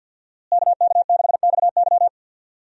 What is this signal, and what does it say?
Morse code, heard as a keyed tone, about 49 wpm, 700 Hz. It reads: KK6XY
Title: USA Call Signs - 5 Characters - Repeated 2x - 50wpm